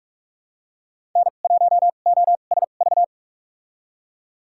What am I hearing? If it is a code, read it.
Morse code, heard as a keyed tone, about 46 wpm, spaced as for 33 wpm, 700 Hz. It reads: N1OSV
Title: USA Call Signs - 5 Characters - 2x Character Spacing - 45wpm